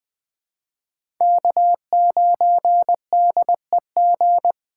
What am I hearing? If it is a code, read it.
Morse code, heard as a keyed tone, about 20 wpm, 700 Hz. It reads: K9DEG
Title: 2023 POTA Hunter Call Signs - 20wpm